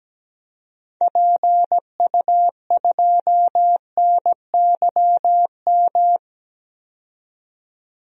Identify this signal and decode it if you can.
Morse code, heard as a keyed tone, about 17 wpm, 700 Hz. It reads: PU2NYM
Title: DXCC-100 Call Signs - 6 Characters - 17wpm